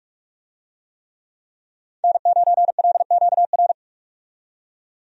Morse code, heard as a keyed tone, about 46 wpm, 700 Hz. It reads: N9LQR
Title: USA Call Signs - 5 Characters - 45wpm